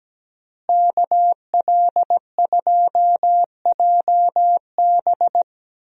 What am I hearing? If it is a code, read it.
Morse code, heard as a keyed tone, about 17 wpm, 700 Hz. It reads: KL2JB
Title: USA Call Signs - 5 Characters - Repeated 3x - 17wpm